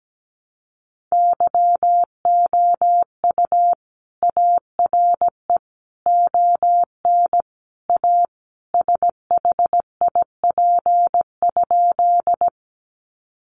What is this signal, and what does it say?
Morse code, heard as a keyed tone, about 17 wpm, 700 Hz. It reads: YOU ARE ON A SHIP?
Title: Sentences from Top 400 Words 17wpm